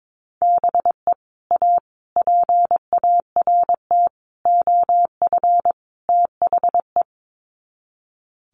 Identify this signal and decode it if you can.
Morse code, heard as a keyed tone, about 22 wpm, 700 Hz. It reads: BE A PART OF THE
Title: Sets of 5 Words Encore 22wpm